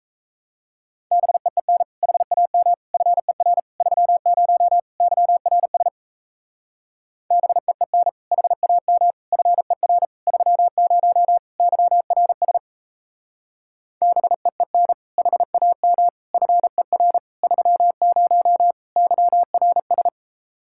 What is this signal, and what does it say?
Morse code, heard as a keyed tone, about 37 wpm, 700 Hz. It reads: BEEN HAM FER 30 YRS BEEN HAM FER 30 YRS BEEN HAM FER 30 YRS